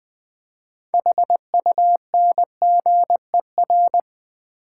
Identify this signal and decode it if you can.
Morse code, heard as a keyed tone, about 20 wpm, 700 Hz. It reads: HUNGER
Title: Build A Word - 2 Letters At A Time - 6-Letter Words - 20wpm